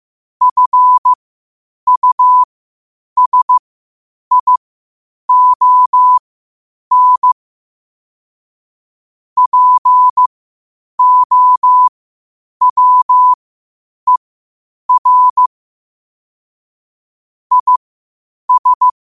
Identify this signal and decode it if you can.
Morse code, heard as a keyed tone, about 15 wpm, spaced as for 8 wpm, 1000 Hz. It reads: FUSION POWER IS
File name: qotdfiles / 8wpm